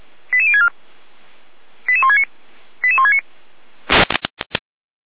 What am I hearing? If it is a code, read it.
Five-tone selective call (ZVEI), heard as quick runs of five tones, tones of about 70 ms. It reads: 88064 88169 88169